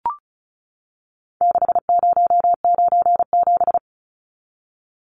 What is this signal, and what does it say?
Morse code, heard as a keyed tone, about 34 wpm, 700 Hz. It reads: 6097
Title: Numbers - 4-Digit 35wpm